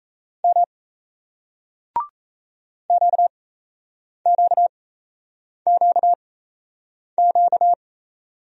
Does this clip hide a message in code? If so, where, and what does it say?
Yes: Morse code from 0.4 to 7.7 s, heard as a keyed tone, about 38 wpm, spaced as for 8 wpm, 700 Hz. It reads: M QQQQ